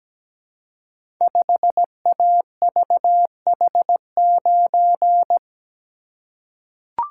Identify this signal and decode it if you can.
Morse code, heard as a keyed tone, about 17 wpm, 700 Hz. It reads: 5AVH9
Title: Code Groups - 5 Letter:Numbers - 17wpm